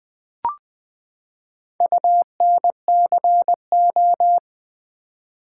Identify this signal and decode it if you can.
Morse code, heard as a keyed tone, about 21 wpm, 700 Hz. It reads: UNCO
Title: Build A Word - 4 Letters At A Time - 11-Letter Words - 20wpm